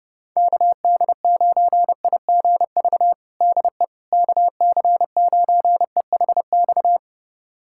Morse code, heard as a keyed tone, about 30 wpm, 700 Hz. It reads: KD9IGV DE KC9EHX
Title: USA Callsign DE Callsign - 6 Characters - 30wpm